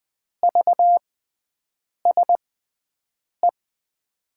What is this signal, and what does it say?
Morse code, heard as a keyed tone, about 20 wpm, spaced as for 7 wpm, 700 Hz. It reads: VSE
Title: USA Call Signs - 5 Characters - 6x Character Spacing - 20wpm